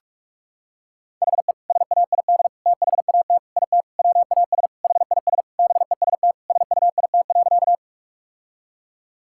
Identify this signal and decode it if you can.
Morse code, heard as a keyed tone, about 46 wpm, 700 Hz. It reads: HE SAID THAT IT WAS HIS BEST SUIT.
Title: Sentences from Top 1600 Words 45wpm